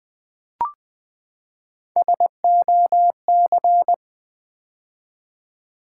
Morse code, heard as a keyed tone, about 20 wpm, 700 Hz. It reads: SOC
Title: Build A Word - 3 Letters At A Time - 6-Letter Words - 20wpm